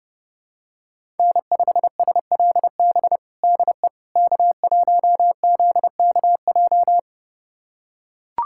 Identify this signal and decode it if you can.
Morse code, heard as a keyed tone, about 32 wpm, 690 Hz. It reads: N5SLB DE K1ZKJ
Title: USA Callsign DE Callsign 30wpm